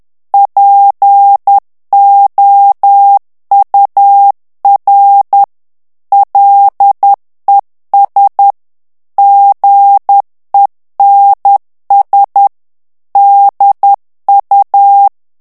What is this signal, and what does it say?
Morse code, heard as a keyed tone, about 10 wpm, 800 Hz. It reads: POUR LES GENS DU
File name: morse mp3 / FRANCAIS TEXTE CLAIR 2 VITESSE 10 MOTS MINUTES